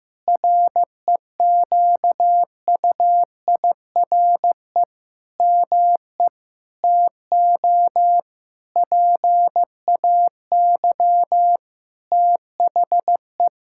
Morse code, heard as a keyed tone, about 15 wpm, 700 Hz. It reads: REQUIRE ME TO PAY THE
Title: Sentences from Top 1000 Words - Review Part 6 - 15wpm